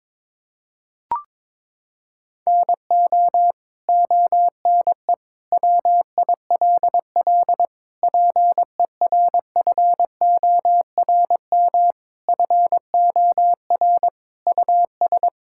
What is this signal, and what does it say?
Morse code, heard as a keyed tone, about 22 wpm, 700 Hz. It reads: NO ONE WILL PERFORM FOR US
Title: Sentences from Top 1000 Words - Review Part 5 - 22wpm